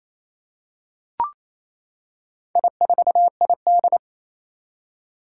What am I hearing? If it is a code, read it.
Morse code, heard as a keyed tone, about 27 wpm, 700 Hz. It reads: I4ID
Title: Four Challenging Letter-Number Combinations 28wpm